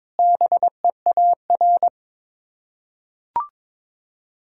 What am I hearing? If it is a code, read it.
Morse code, heard as a keyed tone, about 22 wpm, 700 Hz. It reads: BEAR
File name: practice / 22wpm / Sets of 8 Words - 22wpm